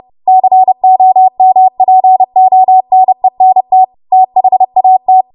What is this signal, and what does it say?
Morse code, heard as a keyed tone, about 30 wpm, 750 Hz. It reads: COMPONENT THAT
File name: Morse / 30WPM